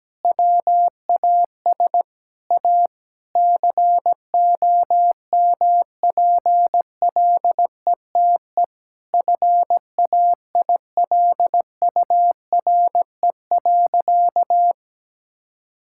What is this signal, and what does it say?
Morse code, heard as a keyed tone, about 17 wpm, 700 Hz. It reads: WAS A COMPLETE FAILURE.
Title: Sentences from Top 1000 Words - Review Part 6 - 17wpm